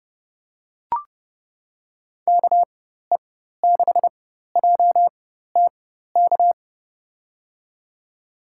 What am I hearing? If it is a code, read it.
Morse code, heard as a keyed tone, about 30 wpm, spaced as for 14 wpm, 700 Hz. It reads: KE6JTK
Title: USA Call Signs - 4x Character Spacing - 30wpm